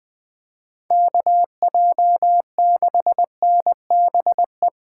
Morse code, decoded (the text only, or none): KJ6NBE